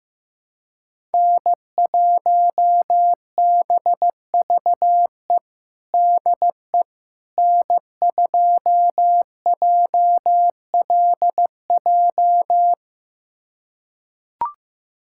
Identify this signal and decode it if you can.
Morse code, heard as a keyed tone, about 15 wpm, 700 Hz. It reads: N1BVE DE N2JLJ
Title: USA Callsign DE Callsign - 5 Characters - 15wpm